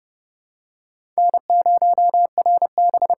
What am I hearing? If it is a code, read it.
Morse code, heard as a keyed tone, about 30 wpm, 700 Hz. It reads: N0RB
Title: USA Call Signs - 5 Characters - 8x Character Spacing - 30wpm